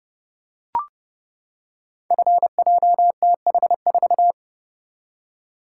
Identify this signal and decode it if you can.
Morse code, heard as a keyed tone, about 30 wpm, 700 Hz. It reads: FJTH4
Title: Code Groups - 5 Letter:Numbers - 30wpm